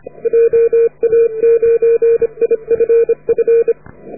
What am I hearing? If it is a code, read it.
Morse code, heard as a keyed tone, about 30 wpm, 440 Hz. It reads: JA9IFF